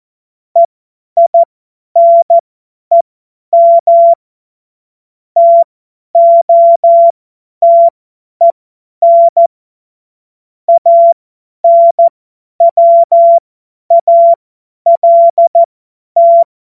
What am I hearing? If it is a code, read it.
Morse code, heard as a keyed tone, about 14 wpm, spaced as for 10 wpm, 680 Hz. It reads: EINEM TOTEN ANWALT
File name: Morsekurs / Witze / Witz 16. August 2022 50 F70